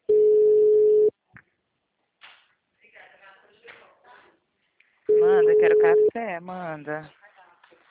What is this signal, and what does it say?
Ringback tone (Europe, 425 Hz), with speech.